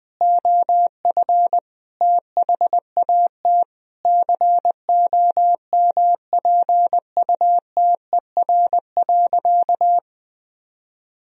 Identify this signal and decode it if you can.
Morse code, heard as a keyed tone, about 19 wpm, 700 Hz. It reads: OF THAT COMPUTER.
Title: Sentences from Top 1000 Words - Review Part 3 - 20wpm